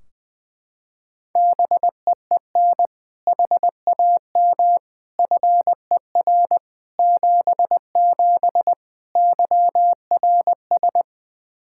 Morse code, heard as a keyed tone, about 20 wpm, 700 Hz. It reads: BEEN HAM FER 77 YRS
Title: Practice QSO - Exchange 3 - More Info - 20wpm